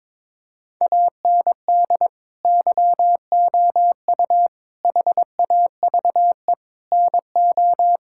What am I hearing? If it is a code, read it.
Morse code, heard as a keyed tone, about 22 wpm, 700 Hz. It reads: AND YOU HAVE NO
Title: Sentences from Top 1000 Words - Review Part 6 - 22wpm